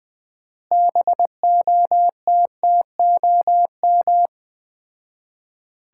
Morse code, heard as a keyed tone, about 21 wpm, 700 Hz. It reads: BOTTOM
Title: 4x Word-Spacing - Sets of 4 Words - 20wpm